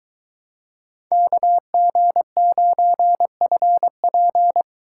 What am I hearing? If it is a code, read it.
Morse code, heard as a keyed tone, about 23 wpm, 700 Hz. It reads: KG9FP